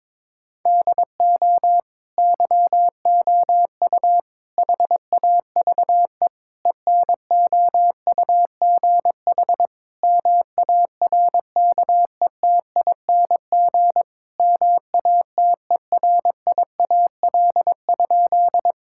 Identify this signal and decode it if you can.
Morse code, heard as a keyed tone, about 22 wpm, 700 Hz. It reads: DO YOU HAVE ENOUGH MARKETING MATERIAL?
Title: Sentences from Top 1000 Words - Review Part 5 - 22wpm